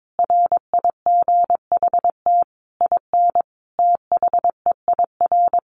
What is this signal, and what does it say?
Morse code, heard as a keyed tone, about 22 wpm, 700 Hz. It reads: RIGHT IN THEIR
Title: Sentences from Top 1300 Words 22wpm